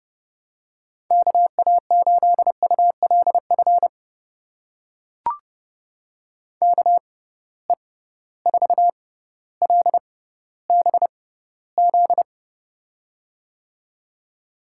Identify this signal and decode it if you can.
Morse code, heard as a keyed tone, about 31 wpm, spaced as for 16 wpm, 700 Hz. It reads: KA8ULF KE4LBZ
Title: USA Call Signs - 6 Characters - 6x Character Spacing - 30wpm